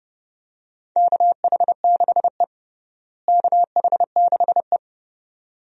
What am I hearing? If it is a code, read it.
Morse code, heard as a keyed tone, about 30 wpm, 700 Hz. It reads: KH6E KH6E
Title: DXCC-100 Call Signs - 4 Characters - Repeated 2x - 30wpm